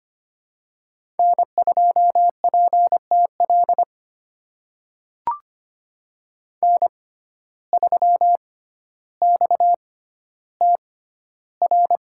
Morse code, heard as a keyed tone, about 25 wpm, spaced as for 13 wpm, 700 Hz. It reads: N2PTL N3XTR
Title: USA Call Signs - 5 Characters - 6x Character Spacing - 25wpm